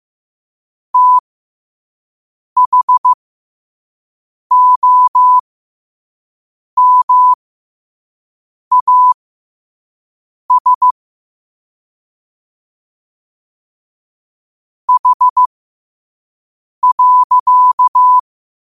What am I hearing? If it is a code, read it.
Morse code, heard as a keyed tone, about 15 wpm, spaced as for 5 wpm, 1000 Hz. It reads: THOMAS H.
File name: qotdfiles / 5wpm